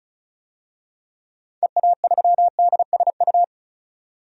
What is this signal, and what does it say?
Morse code, heard as a keyed tone, about 36 wpm, 700 Hz. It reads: EA3DSU
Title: DXCC-100 Call Signs - 6 Characters - 35wpm